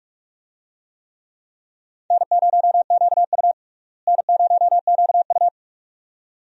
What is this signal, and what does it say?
Morse code, heard as a keyed tone, about 41 wpm, 700 Hz. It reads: N0QU N0QU
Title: USA Call Signs - 4 Characters - Repeated 2x - 45wpm